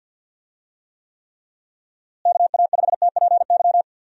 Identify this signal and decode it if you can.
Morse code, heard as a keyed tone, about 51 wpm, 700 Hz. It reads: KA5TPY
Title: USA Call Signs - 6 Characters - Repeated 3x - 50wpm